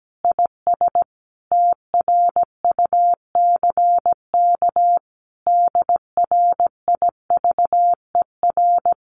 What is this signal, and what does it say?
Morse code, heard as a keyed tone, about 17 wpm, 700 Hz. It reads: IS TRUCK DRIVER
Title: Common QSO Phrases Encore 17wpm